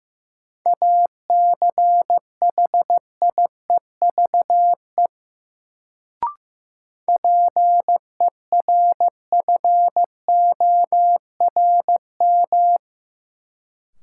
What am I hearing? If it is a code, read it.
Morse code, heard as a keyed tone, about 15 wpm, 700 Hz. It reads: ACHIEVE PERFORM
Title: Sentences from Top 1000 Words - Review Part 5 - New Word Drill - 15wpm